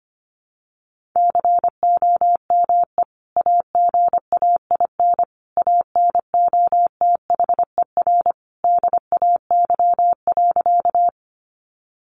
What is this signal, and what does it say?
Morse code, heard as a keyed tone, about 25 wpm, 700 Hz. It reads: COME AGAIN ANOTHER DAY.
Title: Sentences from Top 600 Words 25wpm